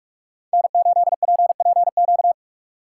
Morse code, heard as a keyed tone, about 45 wpm, 700 Hz. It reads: N8PPQ